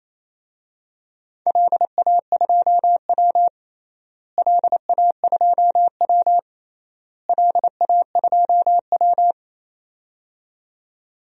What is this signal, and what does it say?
Morse code, heard as a keyed tone, about 28 wpm, 700 Hz. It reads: LA2W LA2W LA2W
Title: DXCC-100 Call Signs - 4 Characters - Repeated 3x - 28wpm